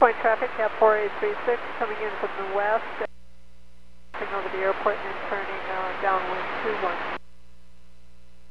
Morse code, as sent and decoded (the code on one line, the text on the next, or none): none